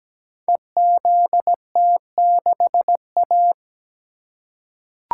Morse code, heard as a keyed tone, about 17 wpm, 700 Hz. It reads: EZT6A
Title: Code Groups - 5 Letter:Numbers - 17wpm